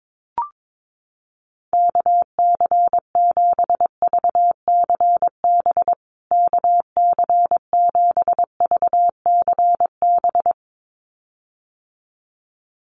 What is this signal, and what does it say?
Morse code, heard as a keyed tone, about 22 wpm, 700 Hz. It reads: KC7VCB KC7VCB
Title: USA Call Signs Repeated 2x 22wpm